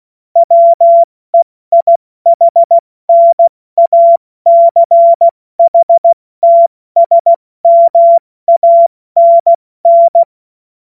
Morse code, heard as a keyed tone, about 16 wpm, 680 Hz. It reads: WEIHNACHTSMANN
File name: Morsekurs / Witze / Witz 16. November 2021, 70 F80